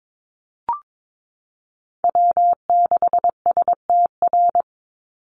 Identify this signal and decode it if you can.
Morse code, heard as a keyed tone, about 22 wpm, 700 Hz. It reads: W6STR
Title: NA SOTA Chaser Callsigns 22wpm